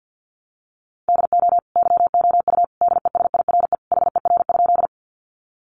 Morse code, heard as a keyed tone, about 50 wpm, 700 Hz. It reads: DO YOU DESIRE HER?
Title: Sentences from Top 1100 Words 50wpm